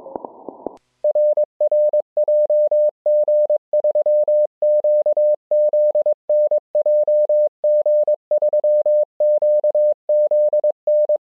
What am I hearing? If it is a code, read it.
Morse code, heard as a keyed tone, about 22 wpm, 600 Hz. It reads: RR JG3QZN JG3QZN